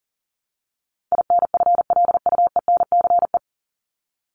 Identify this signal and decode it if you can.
Morse code, heard as a keyed tone, about 40 wpm, 700 Hz. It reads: INFLUENCE
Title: Top 1001-1250 Words 40wpm